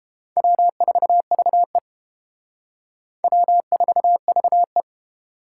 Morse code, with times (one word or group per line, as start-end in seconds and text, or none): 0.4-1.8 W4VE
3.2-4.8 W4VE